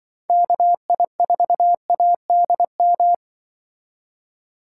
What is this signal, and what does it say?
Morse code, heard as a keyed tone, about 24 wpm, 700 Hz. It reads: KI4ADM
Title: Speed Racing - USA 2x3 Call Signs 20wpm